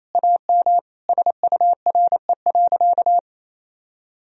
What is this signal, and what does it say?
Morse code, heard as a keyed tone, about 28 wpm, 700 Hz. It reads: AM SURE.
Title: Sentences from Top 1000 Words - Review Part 3 - 28wpm